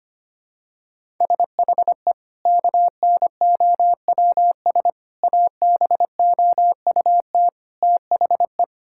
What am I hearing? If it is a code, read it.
Morse code, heard as a keyed tone, about 25 wpm, 700 Hz. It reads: SHE KNOWS ABOUT THE
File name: practice / 25wpm / Sentences with 2-Word Phrases - 25wpm